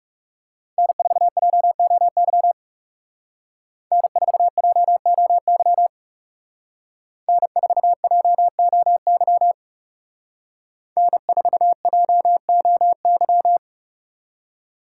Morse code, heard as a keyed tone, about 39 wpm, 700 Hz. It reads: N4JOY N4JOY N4JOY N4JOY